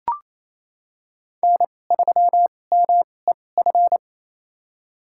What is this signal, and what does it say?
Morse code, heard as a keyed tone, about 28 wpm, spaced as for 20 wpm, 700 Hz. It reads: N3MEF
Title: USA Call Signs - 5 Characters - 2x Character Spacing - 28wpm